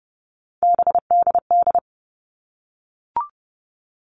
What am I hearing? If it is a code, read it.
Morse code, heard as a keyed tone, about 30 wpm, 700 Hz. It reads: BDD
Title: All Cut Numbers - 3-Digit 30wpm